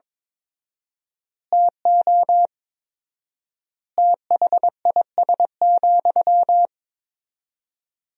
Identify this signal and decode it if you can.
Morse code, heard as a keyed tone, about 21 wpm, 700 Hz. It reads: TO THIS,
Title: 4x Word-Spacing - Sentences from Top 400 Words - 22wpm